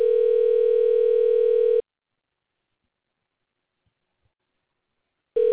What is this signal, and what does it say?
Ringback tone (North America), 440 + 480 Hz.